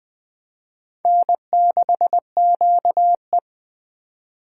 Morse code, heard as a keyed tone, about 23 wpm, 700 Hz. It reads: N6QE
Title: Speed Racing - USA Call Signs 20wpm